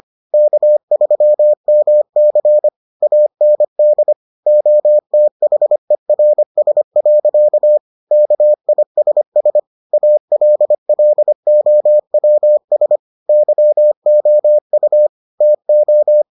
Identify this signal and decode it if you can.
Morse code, heard as a keyed tone, about 25 wpm, 600 Hz. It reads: K3MC AND OTHERS. KISS ALLOWS YOU TO